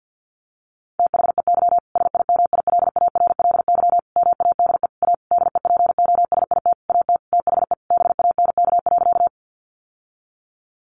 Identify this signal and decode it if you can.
Morse code, heard as a keyed tone, about 50 wpm, 700 Hz. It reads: THEY SIMILARLY MADE A DEPOSIT AT THE BANK.
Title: Sentences from Top 1400 Words 50wpm